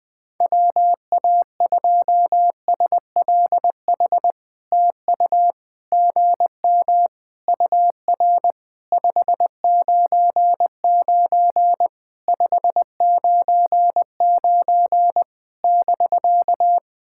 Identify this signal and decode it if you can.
Morse code, heard as a keyed tone, about 20 wpm, 700 Hz. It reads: WA2SLH TU GM UR 599 599 <BK>
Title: POTA QSO Practice - 20wpm